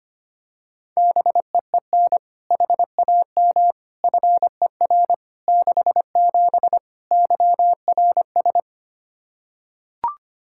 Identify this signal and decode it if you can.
Morse code, heard as a keyed tone, about 25 wpm, 700 Hz. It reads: BEEN HAM FER 67 YRS
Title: Practice QSO - Exchange 3 - More Info - 25wpm